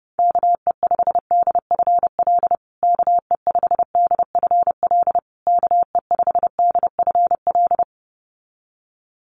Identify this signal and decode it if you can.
Morse code, heard as a keyed tone, about 30 wpm, 700 Hz. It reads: KE5DFL KE5DFL KE5DFL
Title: USA Call Signs Repeated 3x 30wpm